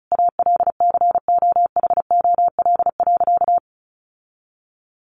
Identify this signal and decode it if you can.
Morse code, heard as a keyed tone, about 34 wpm, 700 Hz. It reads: ALCOHOL.
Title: Sentences from Top 1400 Words 35wpm